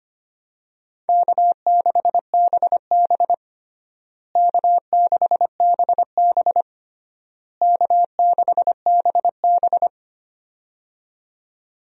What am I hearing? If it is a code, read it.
Morse code, heard as a keyed tone, about 25 wpm, 700 Hz. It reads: K6BB K6BB K6BB
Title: USA Call Signs - 4 Characters - Repeated 3x - 25wpm